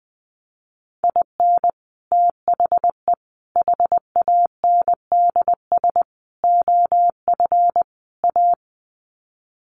Morse code, heard as a keyed tone, about 20 wpm, 700 Hz. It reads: IN THE HANDS OF A
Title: Sets of 5 Words Encore 20wpm